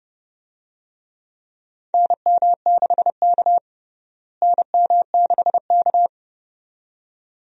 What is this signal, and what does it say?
Morse code, heard as a keyed tone, about 30 wpm, 700 Hz. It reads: NM6K NM6K
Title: USA Call Signs - 4 Characters - Repeated 2x - 30wpm